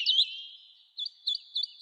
There is no code.